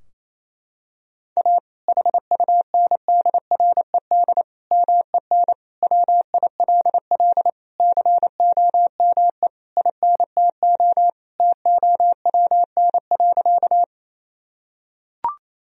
Morse code, heard as a keyed tone, about 28 wpm, 700 Hz. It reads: A HUNDRED MEN WILL COME INTO TOWN.
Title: Sentences from Top 1000 Words - Review Part 2 - 28wpm